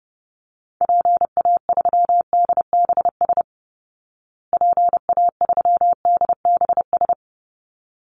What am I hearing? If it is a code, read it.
Morse code, heard as a keyed tone, about 30 wpm, 700 Hz. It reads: PA3DBS PA3DBS
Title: Call Signs Repeated 3x - 30wpm